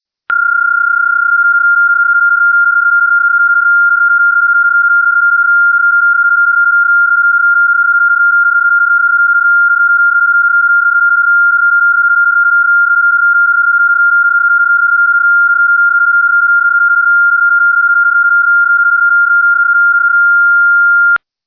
PSK31, about 31.25 baud.